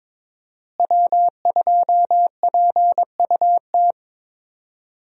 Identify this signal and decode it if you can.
Morse code, heard as a keyed tone, about 22 wpm, 700 Hz. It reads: W2PUT
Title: USA Call Signs - 5 Characters - Repeated 2x - 22wpm